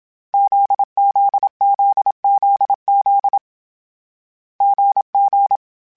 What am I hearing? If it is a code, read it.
Morse code, heard as a keyed tone, about 27 wpm, 800 Hz. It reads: ZZZZZ GG
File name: Difficult Combos / Difficult Combos-mp3s / GZ GZ7 @25